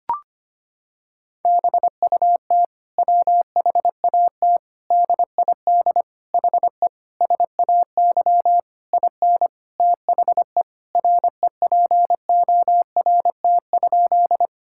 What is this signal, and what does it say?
Morse code, heard as a keyed tone, about 25 wpm, 700 Hz. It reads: BUT WHAT DID HE SAY IN THE REPORT?
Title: Sentences from Top 1000 Words - Review Part 6 - 25wpm